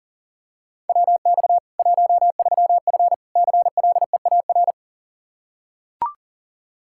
Morse code, heard as a keyed tone, about 40 wpm, 700 Hz. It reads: WX 13F CLEAR
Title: K1USN SST QSO Practice 40wpm